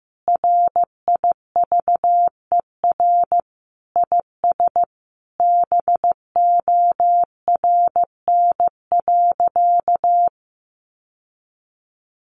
Morse code, heard as a keyed tone, about 15 wpm, 700 Hz. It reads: RIVER IS BORN.
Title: Sentences from Top 1200 Words 15wpm